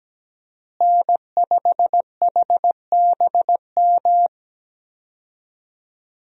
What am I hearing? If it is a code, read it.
Morse code, heard as a keyed tone, about 17 wpm, 700 Hz. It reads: N5HBM
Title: USA Call Signs - 5 Characters - Repeated 3x - 17wpm